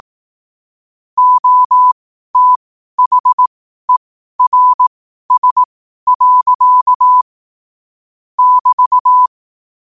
Morse code, heard as a keyed tone, about 18 wpm, spaced as for 12 wpm, 1000 Hz. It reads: OTHERS. <BT>
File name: qotdfiles / 12wpm